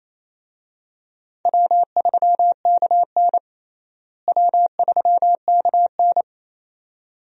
Morse code, heard as a keyed tone, about 28 wpm, 700 Hz. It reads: W3KN W3KN
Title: USA Call Signs - 4 Characters - Repeated 2x - 28wpm